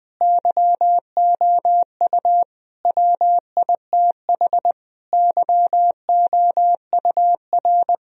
Morse code, heard as a keyed tone, about 20 wpm, 700 Hz. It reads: YOU WITH YOUR
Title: Sentences from Top 1000 Words - Review Part 3 - 20wpm